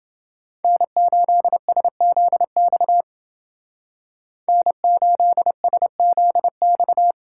Morse code, heard as a keyed tone, about 31 wpm, 700 Hz. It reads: N8SZX N8SZX